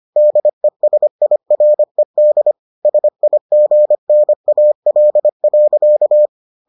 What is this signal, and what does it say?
Morse code, heard as a keyed tone, about 25 wpm, 600 Hz. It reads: DESIRED SIGNAL.